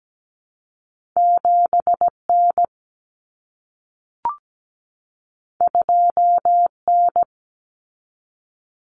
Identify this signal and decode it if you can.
Morse code, heard as a keyed tone, about 17 wpm, 700 Hz. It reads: 7N 2N
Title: Common Cut Numbers - 2-Digit 17wpm